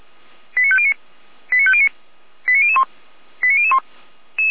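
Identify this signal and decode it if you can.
Five-tone selective call (ZVEI), heard as quick runs of five tones, tones of about 70 ms. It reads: 89509 89509 89001 89001
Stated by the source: ZVEI five-tone call received by SDR, fire brigade Orlamünde.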